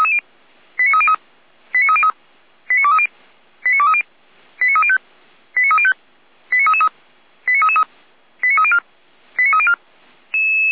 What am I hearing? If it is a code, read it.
Five-tone selective call (ZVEI), heard as quick runs of five tones, tones of about 70 ms. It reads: 89303 89303 89230 89230 89305 89305 89333 89304 89304
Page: /455/SDR/ZVEI/89315/2023.02.22/